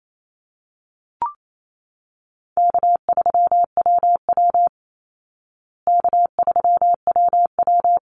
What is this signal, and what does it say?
Morse code, heard as a keyed tone, about 28 wpm, 700 Hz. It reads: K3WW K3WW
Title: Easy Call Signs Repeated 2x - 28wpm